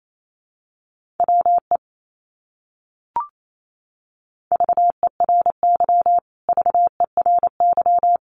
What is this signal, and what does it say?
Morse code, heard as a keyed tone, about 28 wpm, 700 Hz. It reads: WE VERY VERY
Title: Top 200 Words Repeated 28wpm